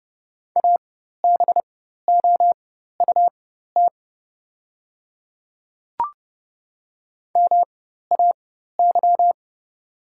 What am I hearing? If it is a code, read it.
Morse code, heard as a keyed tone, about 30 wpm, spaced as for 14 wpm, 700 Hz. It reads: ABOUT MAY
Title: 4x Character Spacing - Top 100 Words - 30wpm